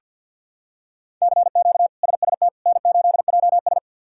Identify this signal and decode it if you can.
Morse code, heard as a keyed tone, about 47 wpm, 700 Hz. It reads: CQ SST N8JS
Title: K1USN SST QSO Practice - 50wpm